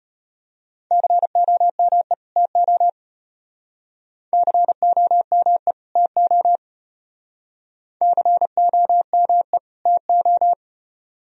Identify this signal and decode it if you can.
Morse code, heard as a keyed tone, about 33 wpm, 700 Hz. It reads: COME TO COME TO COME TO